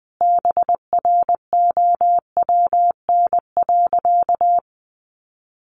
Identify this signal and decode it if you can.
Morse code, heard as a keyed tone, about 19 wpm, 700 Hz. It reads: BROWN.
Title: Sentences from Top 1200 Words 20wpm